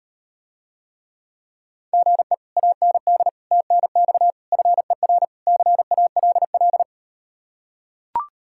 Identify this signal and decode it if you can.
Morse code, heard as a keyed tone, about 38 wpm, 700 Hz. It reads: GE AND TNX FER CALL